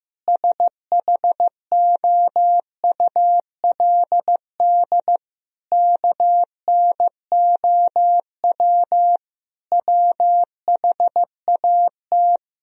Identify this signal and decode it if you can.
Morse code, heard as a keyed tone, about 16 wpm, 700 Hz. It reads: SHOULD KNOW WHAT
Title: Sentences from Top 1000 Words - Review Part 1 - 15wpm